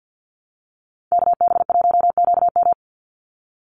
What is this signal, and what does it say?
Morse code, heard as a keyed tone, about 50 wpm, 700 Hz. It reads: KB1QM
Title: USA Call Signs 50wpm